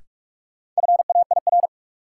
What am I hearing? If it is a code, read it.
Morse code, heard as a keyed tone, about 41 wpm, 700 Hz. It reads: FAIR
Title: Sentences from Top 1000 Words - Review Part 5 - New Word Drill - 45wpm